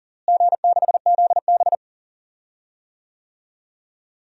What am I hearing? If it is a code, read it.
Morse code, heard as a keyed tone, about 40 wpm, 700 Hz. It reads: G6ZB
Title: DXCC-100 Call Signs - 4 Characters - Repeated 3x - 40wpm